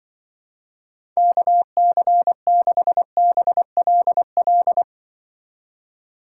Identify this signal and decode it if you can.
Morse code, heard as a keyed tone, about 24 wpm, 700 Hz. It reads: KC6BLL
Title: Speed Racing - USA Call Signs 20wpm